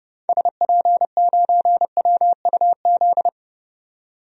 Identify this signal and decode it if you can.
Morse code, heard as a keyed tone, about 30 wpm, 700 Hz. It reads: SP9WUZ
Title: DXCC-100 Call Signs - 6 Characters - 30wpm